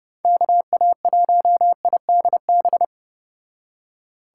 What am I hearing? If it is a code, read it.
Morse code, heard as a keyed tone, about 30 wpm, 700 Hz. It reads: KA1IDB